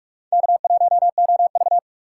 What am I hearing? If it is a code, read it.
Morse code, heard as a keyed tone, about 45 wpm, 700 Hz. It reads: K1OV
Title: USA Call Signs - 4 Characters - 2x Character Spacing - 45wpm